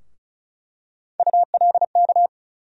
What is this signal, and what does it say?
Morse code, heard as a keyed tone, about 34 wpm, 700 Hz. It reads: ULK
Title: USA Call Sign Postfixes - 3 Letters 35wpm